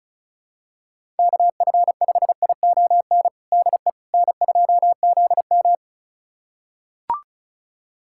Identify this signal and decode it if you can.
Morse code, heard as a keyed tone, about 36 wpm, 700 Hz. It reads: KF5ION DE N2ZM
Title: USA Callsign DE Callsign 35wpm